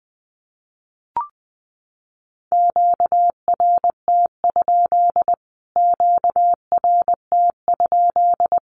Morse code, heard as a keyed tone, about 19 wpm, 700 Hz. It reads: QRT? QRT?
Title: Q-Codes Repeated 3x 20wpm